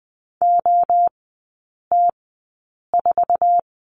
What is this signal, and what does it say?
Morse code, heard as a keyed tone, about 20 wpm, spaced as for 8 wpm, 690 Hz. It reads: OT4
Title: 212 Lesson 34e - Number 8 - Three Characters 20wpm